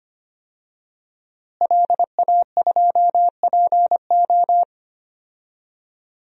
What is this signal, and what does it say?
Morse code, heard as a keyed tone, about 25 wpm, 700 Hz. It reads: LA2PO
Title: DXCC-100 Call Signs - 5 Characters - Repeated 3x - 25wpm